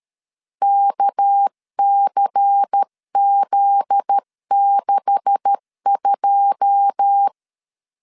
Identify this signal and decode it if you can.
Morse code, heard as a keyed tone, about 13 wpm, 800 Hz. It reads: KCZ62
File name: LEZIONI / 9 286 12wpm 2305